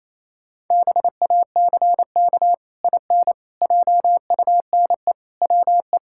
Morse code, heard as a keyed tone, about 28 wpm, 700 Hz. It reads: BACK IN JUNE WE
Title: Pangram Challenge v2 - 28wpm